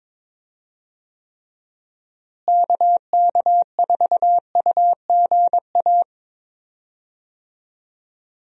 Morse code, heard as a keyed tone, about 22 wpm, 700 Hz. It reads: KK4UGA
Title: USA Call Signs - 6 Characters - Repeated 3x - 22wpm